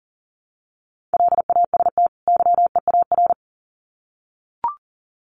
Morse code, heard as a keyed tone, about 40 wpm, 700 Hz. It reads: LAST YEAR
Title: Sets of 2 Words 40wpm